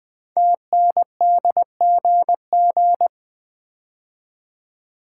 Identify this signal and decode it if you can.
Morse code, heard as a keyed tone, about 20 wpm, 700 Hz. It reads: TNDGG
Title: Code Groups - 5 Letters - 20wpm